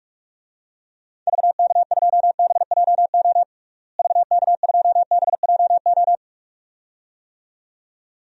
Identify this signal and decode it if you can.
Morse code, heard as a keyed tone, about 41 wpm, 700 Hz. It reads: VK2BJO VK2BJO
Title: DXCC-100 Call Signs - 6 Characters - Repeated 2x - 45wpm